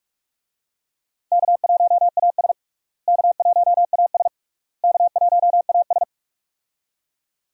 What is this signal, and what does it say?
Morse code, heard as a keyed tone, about 42 wpm, 700 Hz. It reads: K1AS K1AS K1AS
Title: USA Call Signs - 4 Characters - Repeated 3x - 45wpm